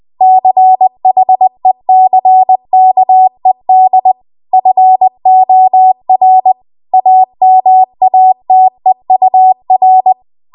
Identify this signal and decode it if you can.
Morse code, heard as a keyed tone, about 20 wpm, 750 Hz. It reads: CHECKED FOR AMATEUR
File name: Archive / 20 WPM / 20WPM